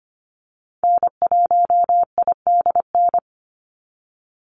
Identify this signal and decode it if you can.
Morse code, heard as a keyed tone, about 25 wpm, 700 Hz. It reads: N1IDN
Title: NA SOTA Activator Callsigns 25wpm